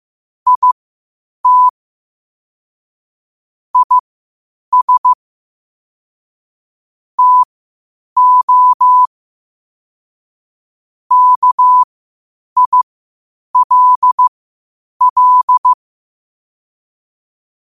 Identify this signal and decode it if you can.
Morse code, heard as a keyed tone, about 15 wpm, spaced as for 8 wpm, 1000 Hz. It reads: IT IS TO KILL
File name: qotdfiles / 8wpm